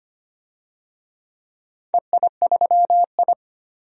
Morse code, heard as a keyed tone, about 25 wpm, 700 Hz. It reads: EI3I
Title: DXCC-100 Call Signs - 4 Characters - 25wpm